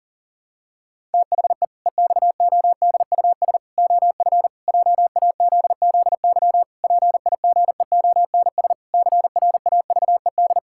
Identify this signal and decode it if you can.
Morse code, heard as a keyed tone, about 37 wpm, 700 Hz. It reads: THE EXODUS OF JAZZY PIGEONS CRAVED
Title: Pangram Challenge v2 - 40wpm